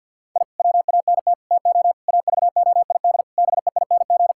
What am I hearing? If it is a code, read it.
Morse code, heard as a keyed tone, about 50 wpm, 700 Hz. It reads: I WANT TO AVOID BEING
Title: Sentences from Top 1000 Words - Review Part 6 - 50wpm